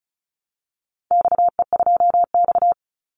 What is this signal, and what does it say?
Morse code, heard as a keyed tone, about 34 wpm, 700 Hz. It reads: XE2X
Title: Call Signs Repeated 3x - 35wpm